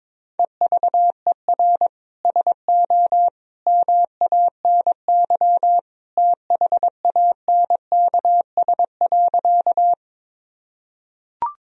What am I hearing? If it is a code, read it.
Morse code, heard as a keyed tone, about 22 wpm, 700 Hz. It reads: EVER SO MANY THANKS.
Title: Sentences from Top 1000 Words - Review Part 3 - 22wpm